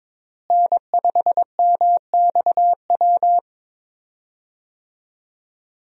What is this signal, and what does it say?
Morse code, heard as a keyed tone, about 22 wpm, 700 Hz. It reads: N5MXW
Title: USA Call Signs - 5 Characters - Repeated 3x - 22wpm